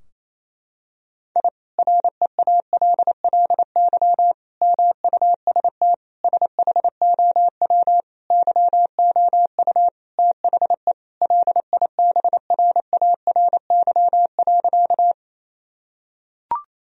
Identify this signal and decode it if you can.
Morse code, heard as a keyed tone, about 27 wpm, 700 Hz. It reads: I REALLY MUST SHOW YOU THE LIBRARY.
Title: Sentences from Top 1000 Words - Review Part 4 - 28wpm